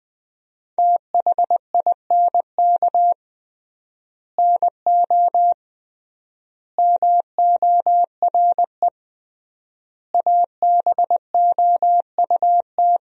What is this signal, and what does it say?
Morse code, heard as a keyed tone, about 21 wpm, 700 Hz. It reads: THINK NO MORE ABOUT
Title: 3x Word-Spacing - Sentences from Top 200 Words - 20wpm